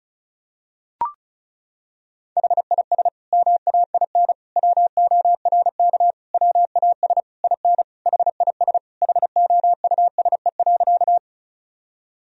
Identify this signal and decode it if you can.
Morse code, heard as a keyed tone, about 36 wpm, 700 Hz. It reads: HIS MAIN WORK WAS IN HIS HOUSE.
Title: Sentences from Top 1000 Words - Review Part 2 - 35wpm